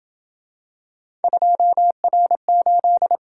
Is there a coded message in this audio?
Yes: Morse code from 1.2 to 3.2 s, heard as a keyed tone, about 28 wpm, 700 Hz. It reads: 2R8